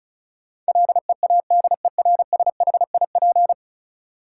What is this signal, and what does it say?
Morse code, heard as a keyed tone, about 36 wpm, 700 Hz. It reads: LEADERSHIP
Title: Sentences from Top 1600 Words - New Word Drill - 35wpm